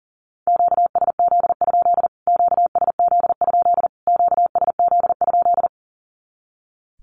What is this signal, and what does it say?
Morse code, heard as a keyed tone, about 40 wpm, 700 Hz. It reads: QSZ? QSZ? QSZ?
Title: Q-Codes Repeated 3x 40wpm